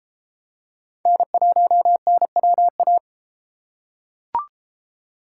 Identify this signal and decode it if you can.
Morse code, heard as a keyed tone, about 32 wpm, 700 Hz. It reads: N1NWA